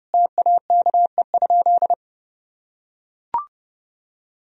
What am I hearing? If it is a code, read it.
Morse code, heard as a keyed tone, about 30 wpm, 700 Hz. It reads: TAKE?
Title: Sentences from Top 1000 Words - Review Part 1 - 30wpm